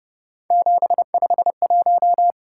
Morse code, heard as a keyed tone, about 29 wpm, 700 Hz. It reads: 751